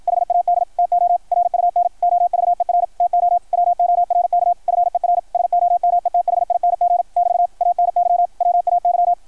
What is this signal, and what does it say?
Morse code, heard as a keyed tone, about 53 wpm, 690 Hz, with radio noise. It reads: HAD TO RUN OVER TO WORK FER SOMETHING <BT> ANY WAY
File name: archived / CRSnet